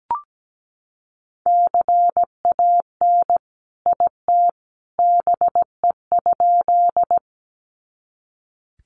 Morse code, heard as a keyed tone, about 17 wpm, 700 Hz. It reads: CAN IT BE?
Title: Sentences from Top 100 Words 17wpm